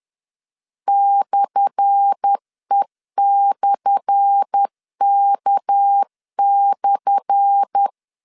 Morse code, heard as a keyed tone, about 10 wpm, 800 Hz. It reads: /E/K/
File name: LEZIONI / 10 SIGNS / 10A SLASH 10wpm 2808